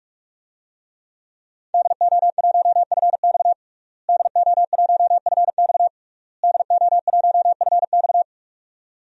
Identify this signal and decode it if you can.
Morse code, heard as a keyed tone, about 46 wpm, 700 Hz. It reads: DO1FX DO1FX DO1FX
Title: DXCC-100 Call Signs - 5 Characters - Repeated 3x - 45wpm